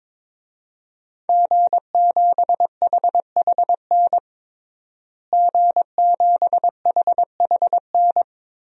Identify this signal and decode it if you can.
Morse code, heard as a keyed tone, about 22 wpm, 700 Hz. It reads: G7HHN G7HHN
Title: DXCC-100 Call Signs - 5 Characters - Repeated 2x - 22wpm